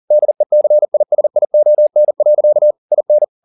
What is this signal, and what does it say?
Morse code, heard as a keyed tone, about 40 wpm, 600 Hz. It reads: DECISION. IN
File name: the old man and the sea 2 40wpm